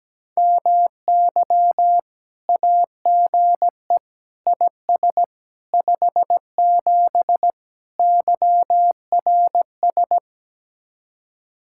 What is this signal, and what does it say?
Morse code, heard as a keyed tone, about 17 wpm, 700 Hz. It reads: MY AGE IS 57 YRS
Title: Speed-Racing Common QSO Phrases - 17wpm